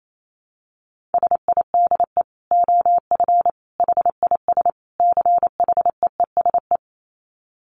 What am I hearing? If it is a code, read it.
Morse code, heard as a keyed tone, about 28 wpm, 700 Hz. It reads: SIDE OF HIS CHEESE
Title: Dit Challenge 28wpm